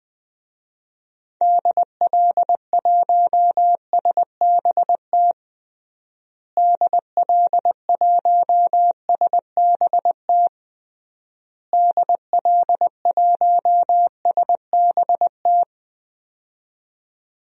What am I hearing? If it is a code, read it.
Morse code, heard as a keyed tone, about 20 wpm, 700 Hz. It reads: DL1SBT DL1SBT DL1SBT
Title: DXCC-100 Call Signs - 6 Characters - Repeated 3x - 20wpm